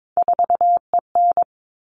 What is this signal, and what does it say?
Morse code, heard as a keyed tone, about 22 wpm, 700 Hz. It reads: 4EN
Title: All Cut Numbers - 3-Digit 22wpm